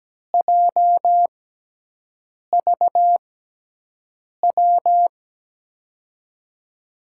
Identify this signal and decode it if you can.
Morse code, heard as a keyed tone, about 17 wpm, spaced as for 6 wpm, 700 Hz. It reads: JVW
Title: USA Call Signs - 6x Character Spacing - 17wpm